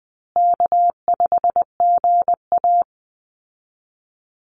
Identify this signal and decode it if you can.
Morse code, heard as a keyed tone, about 20 wpm, 700 Hz. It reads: K5GA
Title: Easy Call Signs Repeated 3x - 20wpm